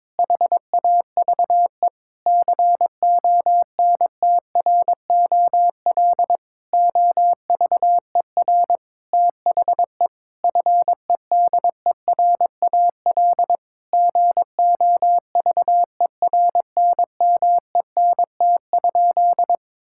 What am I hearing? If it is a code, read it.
Morse code, heard as a keyed tone, about 21 wpm, 700 Hz. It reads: HAVE CONTROL OVER THE FEDERAL GOVERNMENT?
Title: Sentences from Top 1000 Words - Review Part 5 - 22wpm